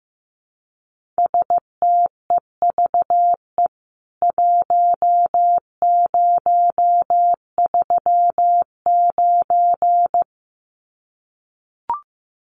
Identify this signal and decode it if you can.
Morse code, heard as a keyed tone, about 15 wpm, 700 Hz. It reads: STEVE 1039
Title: CWT Exchange 15wpm